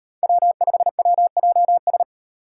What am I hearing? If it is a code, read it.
Morse code, heard as a keyed tone, about 35 wpm, 700 Hz. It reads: W5WJS